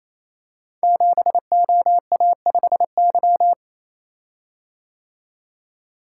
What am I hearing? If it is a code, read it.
Morse code, heard as a keyed tone, about 27 wpm, 700 Hz. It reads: 7OA5Y
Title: Code Groups - 5 Letter:Numbers - 28wpm